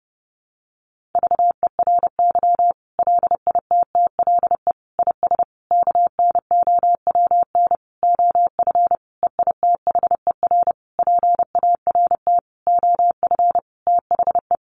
Morse code, heard as a keyed tone, about 30 wpm, 700 Hz. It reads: VERY LITTLE IS KNOWN OF EITHER PART OF THE
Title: Sentences from Top 1500 Words 30wpm